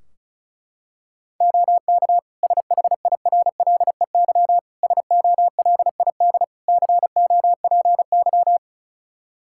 Morse code, heard as a keyed tone, about 36 wpm, 700 Hz. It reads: OK SHIRLEY SOLID COPY
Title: Practice QSO - Exchange 3 - More Info - 35wpm